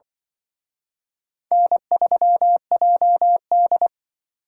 Morse code, heard as a keyed tone, about 25 wpm, 700 Hz. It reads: N3JD
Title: Speed Racing - USA 1x2 Call Signs 20wpm